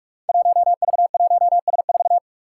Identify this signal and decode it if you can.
Morse code, heard as a keyed tone, about 46 wpm, 700 Hz. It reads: 1V1S4
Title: Five Challenging Letter-Number Combinations 45wpm